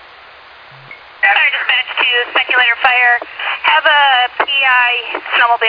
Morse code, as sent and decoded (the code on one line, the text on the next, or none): none